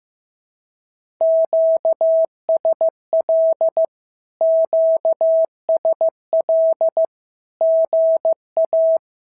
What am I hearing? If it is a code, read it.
Morse code, heard as a keyed tone, about 15 wpm, 650 Hz. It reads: QSL QSL GA
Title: SOTA QSO Practice 15wpm